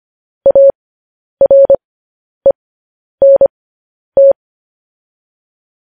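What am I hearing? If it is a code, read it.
Morse code, heard as a keyed tone, about 25 wpm, spaced as for 10 wpm, 550 Hz. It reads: ARENT
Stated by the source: Morse code news headlines at 10.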